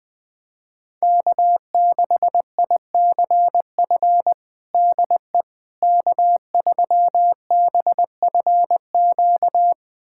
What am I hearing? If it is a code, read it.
Morse code, heard as a keyed tone, about 21 wpm, 700 Hz. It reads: K6ICF DE K3BFQ
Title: USA Callsign DE Callsign - 5 Characters - 20wpm